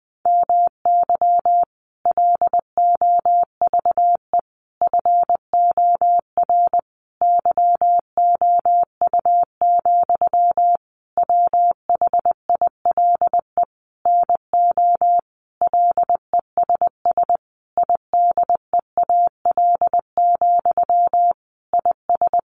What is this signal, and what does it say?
Morse code, heard as a keyed tone, about 20 wpm, 700 Hz. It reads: MY LOVE FOR YOU, WHILE NO LESS IDEAL, IS
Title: Sentences from Top 1400 Words 20wpm